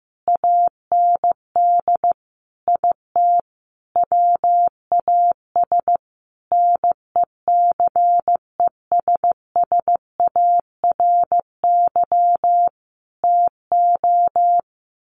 Morse code, heard as a keyed tone, about 15 wpm, 700 Hz. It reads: AND IT WAS NECESSARY TO
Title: Sentences from Top 800 Words 15wpm